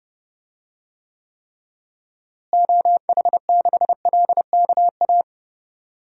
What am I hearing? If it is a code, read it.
Morse code, heard as a keyed tone, about 31 wpm, 700 Hz. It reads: OH6LKA
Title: DXCC-100 Call Signs - 6 Characters - Repeated 3x - 30wpm